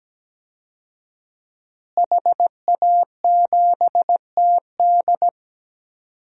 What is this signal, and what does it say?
Morse code, heard as a keyed tone, about 17 wpm, 700 Hz. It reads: HA7TD
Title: DXCC-100 Call Signs - 5 Characters - Repeated 3x - 17wpm